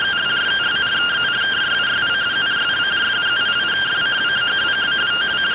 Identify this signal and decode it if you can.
Two-tone teleprinter signal, about 170 Hz shift.